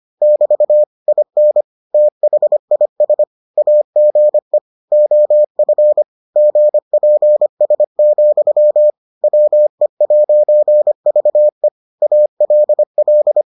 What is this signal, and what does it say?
Morse code, heard as a keyed tone, about 25 wpm, 600 Hz. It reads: <BT> IN THIS AGE OF GPS, WE'VE ALL